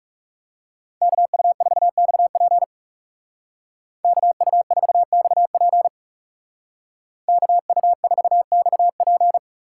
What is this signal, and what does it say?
Morse code, heard as a keyed tone, about 36 wpm, 700 Hz. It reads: KU4XP KU4XP KU4XP
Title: Speed Racing - USA Call Signs 30wpm